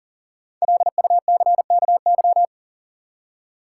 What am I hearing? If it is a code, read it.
Morse code, heard as a keyed tone, about 37 wpm, 700 Hz. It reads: LUCKY
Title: Five Letter Words - 40wpm